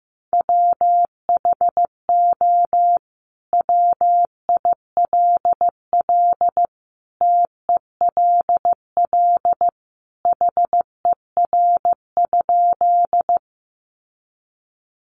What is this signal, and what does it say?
Morse code, heard as a keyed tone, about 15 wpm, 700 Hz. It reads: WHO WILL TELL HER?
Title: Sentences from Top 200 Words 15wpm